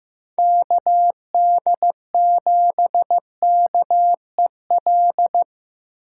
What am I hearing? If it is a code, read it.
Morse code, heard as a keyed tone, about 15 wpm, 700 Hz. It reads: KD7KEL
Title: USA Call Signs - 3x Character Spacing - 15wpm